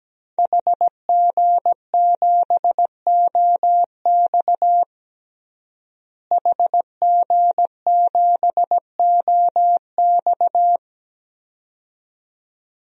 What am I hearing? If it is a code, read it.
Morse code, heard as a keyed tone, about 17 wpm, 700 Hz. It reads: HG7OX HG7OX
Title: DXCC-100 Call Signs - 5 Characters - Repeated 2x - 17wpm